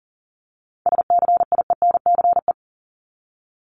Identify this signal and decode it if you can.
Morse code, heard as a keyed tone, about 40 wpm, 700 Hz. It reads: SCIENCE
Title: Top 500 Words 40wpm